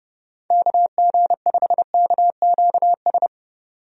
Morse code, heard as a keyed tone, about 30 wpm, 700 Hz. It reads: KG5KQS
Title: USA Call Signs - 6 Characters - 30wpm